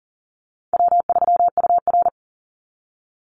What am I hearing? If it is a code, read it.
Morse code, heard as a keyed tone, about 40 wpm, 700 Hz. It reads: W3UR
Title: Easy Call Signs Repeated 3x - 40wpm